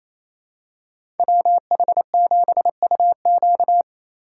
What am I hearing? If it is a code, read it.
Morse code, heard as a keyed tone, about 28 wpm, 700 Hz. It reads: WH7UQ
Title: USA Call Signs - 5 Characters - 4x Character Spacing - 28wpm